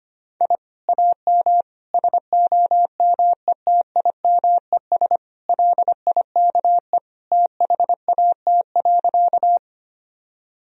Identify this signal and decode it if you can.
Morse code, heard as a keyed tone, about 25 wpm, 700 Hz. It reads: I AM SOMETIMES LIKE THAT.
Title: Sentences from Top 1000 Words - Review Part 3 - 25wpm